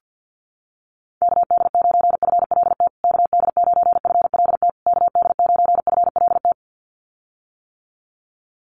Morse code, heard as a keyed tone, about 50 wpm, 700 Hz. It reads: KD9FLT KD9FLT KD9FLT
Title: USA Call Signs Repeated 3x 50wpm